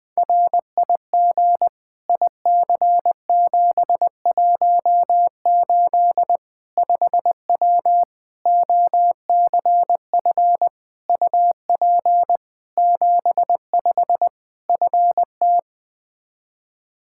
Morse code, heard as a keyed tone, about 21 wpm, 700 Hz. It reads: RIG IC718 5W OCF UP 75 FT
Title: K1USN SST QSO Practice 20wpm